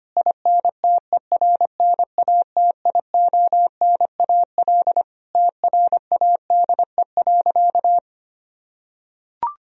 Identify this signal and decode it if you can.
Morse code, heard as a keyed tone, about 25 wpm, 700 Hz. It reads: INTERNATIONAL TRADE.
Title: Sentences from Top 1000 Words - Review Part 4 - 25wpm